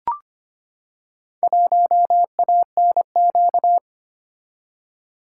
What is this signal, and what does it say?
Morse code, heard as a keyed tone, about 26 wpm, 700 Hz. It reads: 1ANQ
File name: practice / 25wpm / Four Letter-Number Combinations - 25wpm